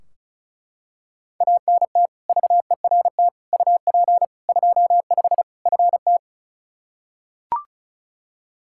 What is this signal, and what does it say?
Morse code, heard as a keyed tone, about 36 wpm, 700 Hz. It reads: ANT VERT UP 25 FT
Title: K1USN SST QSO Practice 35wpm